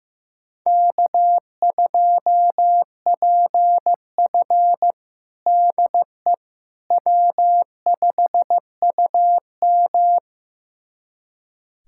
Morse code, heard as a keyed tone, about 15 wpm, 700 Hz. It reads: K2PF DE W5UM
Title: USA Callsign DE Callsign - 4 Characters - 15wpm